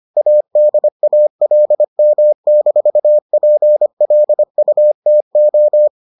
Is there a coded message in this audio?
Yes: Morse code from 0.2 to 5.9 s, heard as a keyed tone, about 25 wpm, 600 Hz. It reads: ADALM-PLUTO